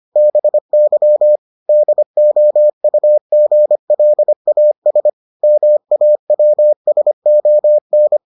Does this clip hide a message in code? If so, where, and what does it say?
Yes: Morse code from 0.2 to 8.2 s, heard as a keyed tone, about 25 wpm, 600 Hz. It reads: BY DOUGLAS MAWSON